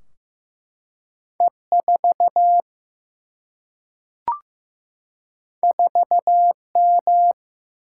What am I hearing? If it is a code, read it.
Morse code, heard as a keyed tone, about 15 wpm, 700 Hz. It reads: E 4 4 M
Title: Single-Letter Single-Number 15wpm